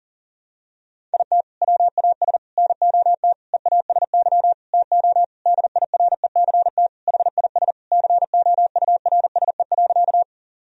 Morse code, heard as a keyed tone, about 40 wpm, 700 Hz. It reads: IT WAS NOT EASY TO DIRECT HIS COURSE.
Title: Sentences from Top 1000 Words - Review Part 2 - 40wpm